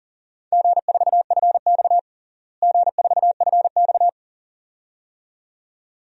Morse code, heard as a keyed tone, about 37 wpm, 700 Hz. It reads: G4FX G4FX
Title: DXCC-100 Call Signs - 4 Characters - Repeated 2x - 40wpm